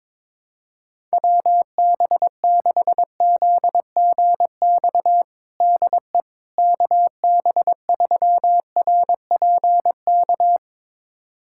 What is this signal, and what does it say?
Morse code, heard as a keyed tone, about 22 wpm, 700 Hz. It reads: WB6ZGX DE KB3RPK
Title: USA Callsign DE Callsign 22wpm